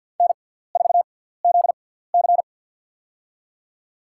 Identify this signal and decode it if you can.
Morse code, heard as a keyed tone, about 53 wpm, spaced as for 17 wpm, 700 Hz. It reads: N4ZC
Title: USA Call Signs - 4 Characters - 6x Character Spacing - 50wpm